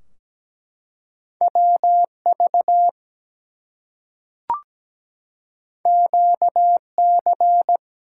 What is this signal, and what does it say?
Morse code, heard as a keyed tone, about 17 wpm, 700 Hz. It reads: WV QC
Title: US States and Canadian Provinces 17wpm